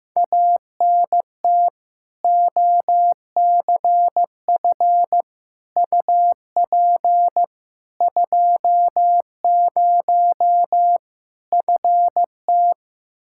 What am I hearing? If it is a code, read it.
Morse code, heard as a keyed tone, about 15 wpm, 700 Hz. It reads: ANT OCF UP 20 FT
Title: Practice QSO - Exchange 2 - Info - 15wpm